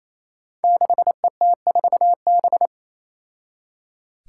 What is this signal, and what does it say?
Morse code, heard as a keyed tone, about 27 wpm, 700 Hz. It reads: 6ET4B
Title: Five Challenging Letter-Number Combinations 28wpm